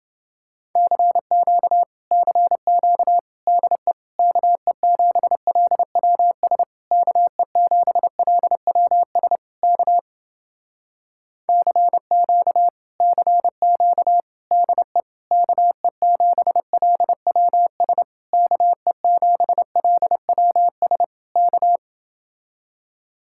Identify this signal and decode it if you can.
Morse code, heard as a keyed tone, about 30 wpm, 700 Hz. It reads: CQ CQ DE KE7LWS KE7LWS K CQ CQ DE KE7LWS KE7LWS K